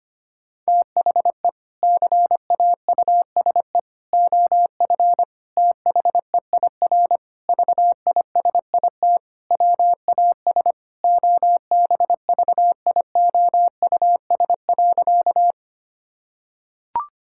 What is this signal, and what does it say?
Morse code, heard as a keyed tone, about 25 wpm, 700 Hz. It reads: THE CAUSE OF THEIR VISIT WAS OBVIOUS.
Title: Sentences from Top 1600 Words 25wpm